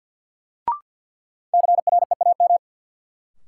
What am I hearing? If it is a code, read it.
Morse code, heard as a keyed tone, about 52 wpm, 700 Hz. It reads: CREAM
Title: Sentences from Top 1500 Words - New Word Drill Alt - 50wpm